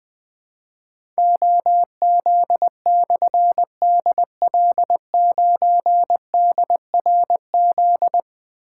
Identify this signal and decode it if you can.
Morse code, heard as a keyed tone, about 21 wpm, 700 Hz. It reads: OZ/DL9DRZ
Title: Difficult Call Signs 20wpm